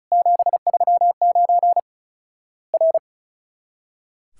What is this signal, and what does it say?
Morse code, heard as a keyed tone, about 36 wpm, 700 Hz. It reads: 739 R
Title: Worked All Europe DX Contest - QTC Exchange - 35wpm